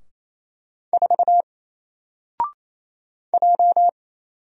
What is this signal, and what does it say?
Morse code, heard as a keyed tone, about 28 wpm, 700 Hz. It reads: 4 J
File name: Single-Letter-Number-28wpm